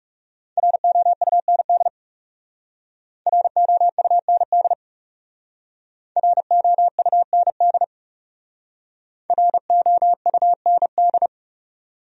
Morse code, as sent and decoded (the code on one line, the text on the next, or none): .-. --- ..- -. -.. / .-. --- ..- -. -.. / .-. --- ..- -. -.. / .-. --- ..- -. -..
ROUND ROUND ROUND ROUND